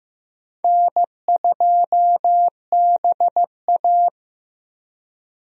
Morse code, heard as a keyed tone, about 15 wpm, 700 Hz. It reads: N2BA
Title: USA Call Signs - 4 Characters - 15wpm